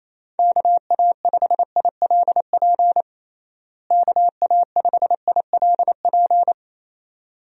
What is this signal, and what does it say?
Morse code, heard as a keyed tone, about 28 wpm, 700 Hz. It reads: KA5ILP KA5ILP
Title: USA Call Signs - 6 Characters - Repeated 2x - 28wpm